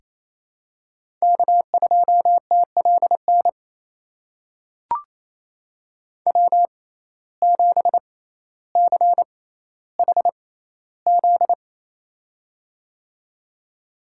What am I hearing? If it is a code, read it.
Morse code, heard as a keyed tone, about 28 wpm, spaced as for 14 wpm, 700 Hz. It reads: K2TLN W7CHZ
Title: USA Call Signs - 5 Characters - 6x Character Spacing - 28wpm